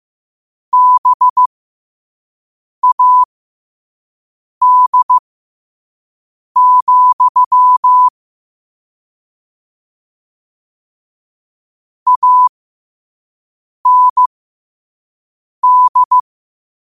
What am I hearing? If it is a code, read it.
Morse code, heard as a keyed tone, about 15 wpm, spaced as for 5 wpm, 1000 Hz. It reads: BAD, AND
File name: qotdfiles / 5wpm